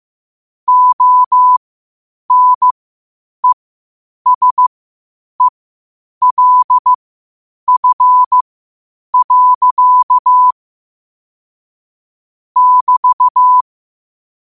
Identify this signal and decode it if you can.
Morse code, heard as a keyed tone, about 15 wpm, spaced as for 8 wpm, 1000 Hz. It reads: ONESELF. <BT>
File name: qotdfiles / 8wpm